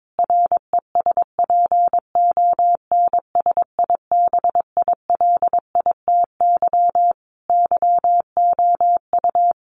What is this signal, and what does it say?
Morse code, heard as a keyed tone, about 22 wpm, 700 Hz. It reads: RESPONSIBILITY YOU
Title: Sentences from Top 1500 Words 22wpm